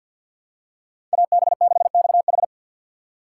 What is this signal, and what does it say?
Morse code, heard as a keyed tone, about 52 wpm, 700 Hz. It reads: AB6XH
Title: USA Call Signs - 5 Characters - 6x Character Spacing - 50wpm